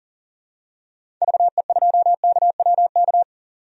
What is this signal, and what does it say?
Morse code, heard as a keyed tone, about 37 wpm, 700 Hz. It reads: VE2KWK
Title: DXCC-100 Call Signs - 6 Characters - 40wpm